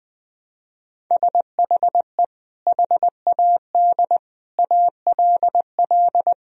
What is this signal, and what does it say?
Morse code, heard as a keyed tone, about 20 wpm, 700 Hz. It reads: SHE HAD ALL
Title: Sentences from Top 1000 Words - Review Part 1 - 20wpm